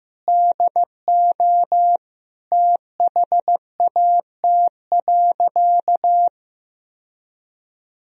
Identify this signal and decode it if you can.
Morse code, heard as a keyed tone, about 16 wpm, 700 Hz. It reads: DO THAT.
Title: Sentences from Top 1000 Words - Review Part 3 - 15wpm